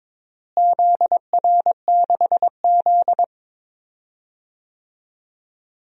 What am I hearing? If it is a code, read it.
Morse code, heard as a keyed tone, about 21 wpm, 700 Hz. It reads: ZR6Z
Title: DXCC-100 Call Signs - 4 Characters - Repeated 2x - 22wpm